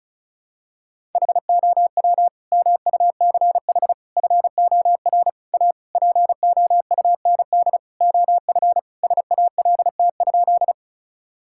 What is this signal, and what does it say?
Morse code, heard as a keyed tone, about 36 wpm, 700 Hz. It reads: HOW MUCH FOR A POUND OF SALT?
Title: Sentences from Top 1000 Words - Review Part 5 - 35wpm